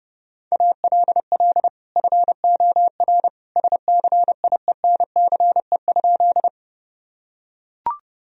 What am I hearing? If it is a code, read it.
Morse code, heard as a keyed tone, about 32 wpm, 700 Hz. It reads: ALL FOR SCIENCE?
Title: Sentences from Top 1000 Words - Review Part 2 - 30wpm